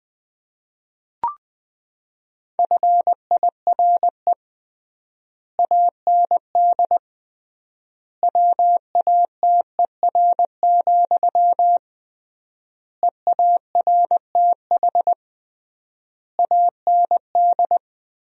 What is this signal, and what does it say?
Morse code, heard as a keyed tone, about 20 wpm, 700 Hz. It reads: FIRE AND WATER, EARTH AND
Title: 3x Word-Spacing - Sentences from Top 400 Words - 20wpm